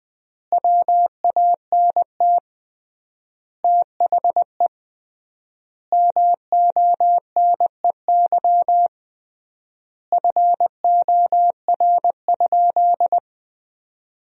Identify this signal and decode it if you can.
Morse code, heard as a keyed tone, about 20 wpm, 700 Hz. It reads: WANT THE MONEY FOR?
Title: 3x Word-Spacing - Sentences from Top 400 Words - 20wpm